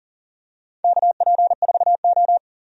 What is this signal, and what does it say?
Morse code, heard as a keyed tone, about 40 wpm, 700 Hz. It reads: KP4O
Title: DXCC-100 Call Signs - 4 Characters - 40wpm